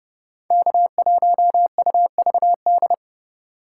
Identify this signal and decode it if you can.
Morse code, heard as a keyed tone, about 29 wpm, 700 Hz. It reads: K1UVD